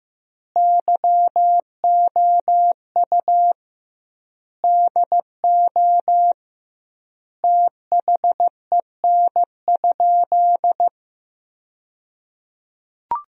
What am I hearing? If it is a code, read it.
Morse code, heard as a keyed tone, about 16 wpm, 700 Hz. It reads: YOU DO THEN?
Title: 2x Word-Spacing - Sentences from Top 100 Words - 15wpm